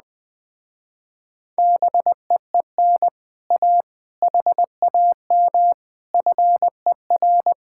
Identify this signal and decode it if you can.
Morse code, heard as a keyed tone, about 20 wpm, 700 Hz. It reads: BEEN A HAM FER
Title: Speed-Racing Common QSO Phrases - 17wpm